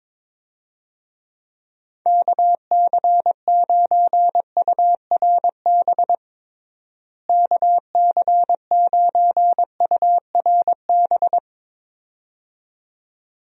Morse code, heard as a keyed tone, about 22 wpm, 700 Hz. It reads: KC9URB KC9URB
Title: USA Call Signs - 6 Characters - Repeated 2x - 22wpm